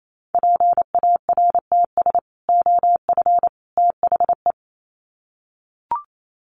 Morse code, heard as a keyed tone, about 28 wpm, 700 Hz. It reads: PARTS OF THE
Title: Sets of 3 Words 28wpm